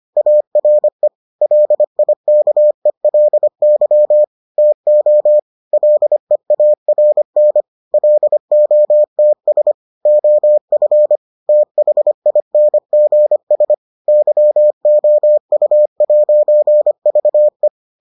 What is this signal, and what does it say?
Morse code, heard as a keyed tone, about 25 wpm, 600 Hz. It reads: ARE LIKELY TO LEARN LOTS OF THINGS YOU'VE